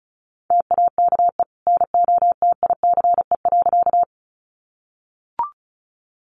Morse code, heard as a keyed tone, about 34 wpm, 700 Hz. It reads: TAKE NOTICE.
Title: Sentences from Top 400 Words 35wpm